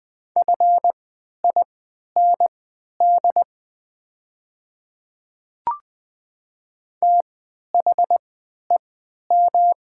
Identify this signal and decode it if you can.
Morse code, heard as a keyed tone, about 20 wpm, spaced as for 11 wpm, 690 Hz. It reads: FIND THEM
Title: 3x Character Spacing - Top 100 Words - 20wpm